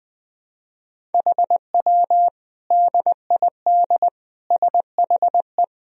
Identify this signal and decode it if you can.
Morse code, heard as a keyed tone, about 20 wpm, 700 Hz. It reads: HW DID SHE
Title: Sentences from Top 100 Words and Abbreviations 20wpm